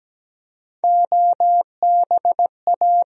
Morse code, heard as a keyed tone, about 17 wpm, 700 Hz. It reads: OBA
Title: USA Call Sign Postfixes - 3 Letters 17wpm